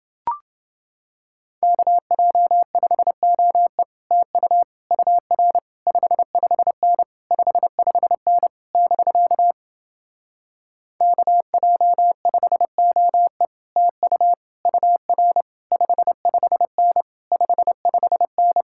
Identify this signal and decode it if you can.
Morse code, heard as a keyed tone, about 29 wpm, 700 Hz. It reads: KJ5OE TU UR 55N 55N <BK> KJ5OE TU UR 55N 55N